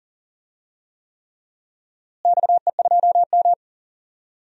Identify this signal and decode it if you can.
Morse code, heard as a keyed tone, about 37 wpm, 700 Hz. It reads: XE2M
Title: DXCC-100 Call Signs - 4 Characters - Repeated 3x - 40wpm